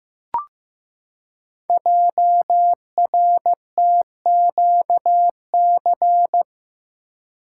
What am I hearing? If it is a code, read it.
Morse code, heard as a keyed tone, about 15 wpm, 700 Hz. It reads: JRTQC
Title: Code Groups - 5 Letters - 15wpm